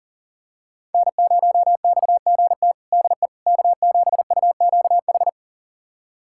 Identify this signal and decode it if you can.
Morse code, heard as a keyed tone, about 40 wpm, 700 Hz. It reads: N0XGT DE K7UQH
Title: USA Callsign DE Callsign - 5 Characters - 40wpm